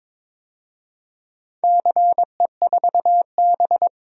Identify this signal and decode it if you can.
Morse code, heard as a keyed tone, about 21 wpm, 700 Hz. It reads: CE4B
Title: DXCC-100 Call Signs - 4 Characters - 22wpm